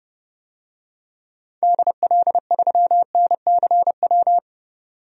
Morse code, heard as a keyed tone, about 30 wpm, 700 Hz. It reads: DL3NCW
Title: DXCC-100 Call Signs - 6 Characters - 30wpm